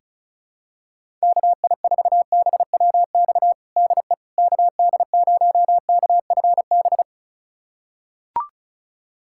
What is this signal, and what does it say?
Morse code, heard as a keyed tone, about 36 wpm, 700 Hz. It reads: KI4BWX DE KD0KFB
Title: USA Callsign DE Callsign - 6 Characters - 35wpm